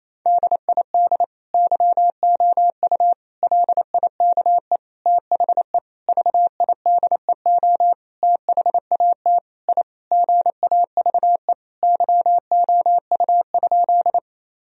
Morse code, heard as a keyed tone, about 27 wpm, 700 Hz. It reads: DID YOU LIKE THE VIDEO THAT I GAVE YOU?
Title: Sentences from Top 1000 Words - Review Part 4 - 28wpm